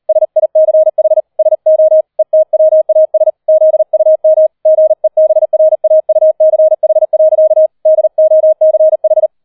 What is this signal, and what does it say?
Morse code, heard as a keyed tone, about 38 wpm, 630 Hz. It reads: SICH SO ETWAS ZUM GEBRAUCH. DOCH